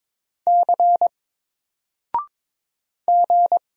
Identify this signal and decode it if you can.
Morse code, heard as a keyed tone, about 22 wpm, 700 Hz. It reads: C G
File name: Single-Letter-Number-22wpm